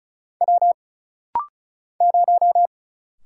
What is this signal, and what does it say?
Morse code, heard as a keyed tone, about 33 wpm, 700 Hz. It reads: W 0
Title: Single Letter-Number - Rapid-Fire 35wpm